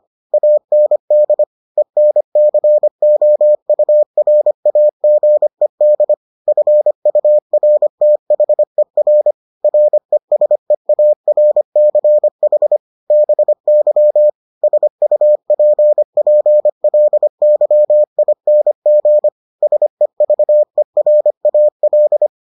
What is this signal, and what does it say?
Morse code, heard as a keyed tone, about 25 wpm, 600 Hz. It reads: AND ENCOURAGED FURTHER RESEARCH BY SUPPLYING SEVERAL